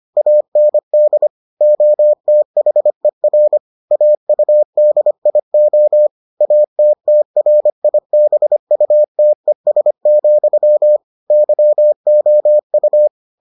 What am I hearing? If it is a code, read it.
Morse code, heard as a keyed tone, about 25 wpm, 600 Hz. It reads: AND OTHER AUDIO ATTRIBUTES, YOU